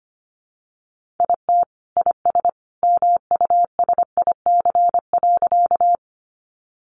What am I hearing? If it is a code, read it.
Morse code, heard as a keyed tone, about 25 wpm, 700 Hz. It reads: IT IS MUSIC.
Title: Sentences from Top 300 Words 25wpm